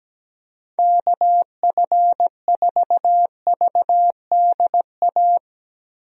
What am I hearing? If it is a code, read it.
Morse code, heard as a keyed tone, about 17 wpm, 700 Hz. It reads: KF4VDA
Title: USA Call Signs - 6 Characters - Repeated 2x - 17wpm